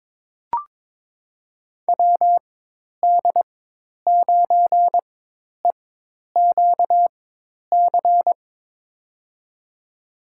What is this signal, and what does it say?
Morse code, heard as a keyed tone, about 22 wpm, spaced as for 10 wpm, 700 Hz. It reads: WD9EQC
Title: USA Call Signs - 6 Characters - 4x Character Spacing - 22wpm